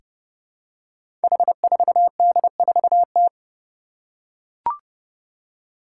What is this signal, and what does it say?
Morse code, heard as a keyed tone, about 28 wpm, 700 Hz. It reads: H4D4T
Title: Five Challenging Letter-Number Combinations 30wpm